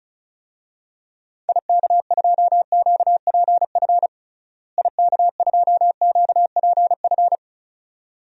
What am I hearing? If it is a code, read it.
Morse code, heard as a keyed tone, about 34 wpm, 700 Hz. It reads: IK2QPF IK2QPF
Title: DXCC-100 Call Signs - 6 Characters - Repeated 2x - 35wpm